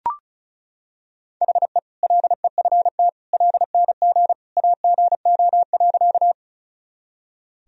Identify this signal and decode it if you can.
Morse code, heard as a keyed tone, about 34 wpm, 700 Hz. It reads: HE LEFT LNG AGO.
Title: Sentences from Top 500 Words and Abbreviations 35wpm